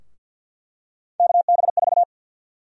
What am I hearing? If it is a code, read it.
Morse code, heard as a keyed tone, about 53 wpm, 700 Hz. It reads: KB4
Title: USA Call Sign Prefixes - 50wpm